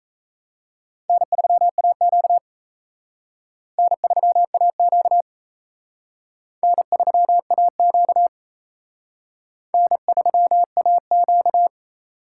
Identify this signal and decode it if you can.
Morse code, heard as a keyed tone, about 34 wpm, 700 Hz. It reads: N3AQ N3AQ N3AQ N3AQ
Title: Speed Racing - USA Call Signs 28wpm